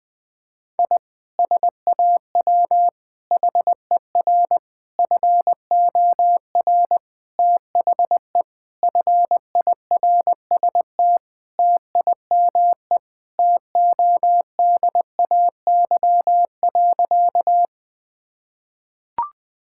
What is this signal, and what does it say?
Morse code, heard as a keyed tone, about 21 wpm, 700 Hz. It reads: I SAW HER FOR THE FIRST TIME TODAY.
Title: Sentences with 4-Word Phrases - 20wpm